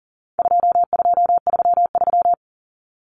Morse code, heard as a keyed tone, about 40 wpm, 700 Hz. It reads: 2233
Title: Common Cut Numbers - 4-Digit 40wpm